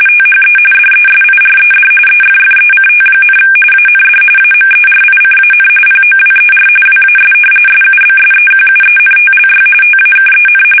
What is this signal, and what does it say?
RTTY, about 45.45 baud. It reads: THE DESIRED EFFECT, AND AMERICAN POWER WAS SUBSEQUENTLY TAKEN
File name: fsk850 / usnavy3